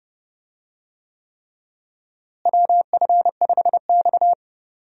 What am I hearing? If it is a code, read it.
Morse code, heard as a keyed tone, about 29 wpm, 700 Hz. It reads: WF5X
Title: USA Call Signs - 4 Characters - Repeated 3x - 30wpm